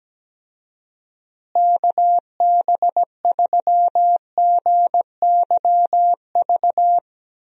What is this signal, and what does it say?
Morse code, heard as a keyed tone, about 17 wpm, 700 Hz. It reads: KB3GYV
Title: USA Call Signs - 6 Characters - Repeated 2x - 17wpm